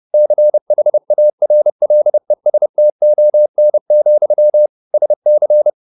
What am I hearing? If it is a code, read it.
Morse code, heard as a keyed tone, about 30 wpm, 600 Hz. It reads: CHARLESTON, SC